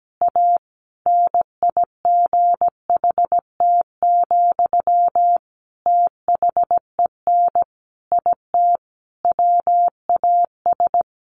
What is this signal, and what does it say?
Morse code, heard as a keyed tone, about 17 wpm, 700 Hz. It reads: A NIGHT, THEN IT WAS
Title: A Sound Of Thunder 17wpm